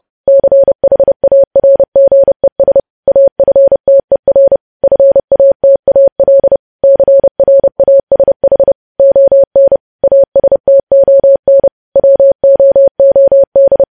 Morse code, heard as a keyed tone, about 30 wpm, 550 Hz. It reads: CHARGES AFTER FATAL CRASH ON ASTON WOOD
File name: podcast / News headlines-1x-30x30